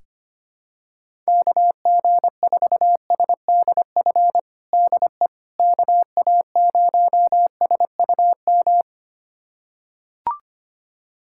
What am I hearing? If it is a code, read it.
Morse code, heard as a keyed tone, about 25 wpm, 700 Hz. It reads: KG4SDF DE KA0SUM
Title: USA Callsign DE Callsign - 6 Characters - 25wpm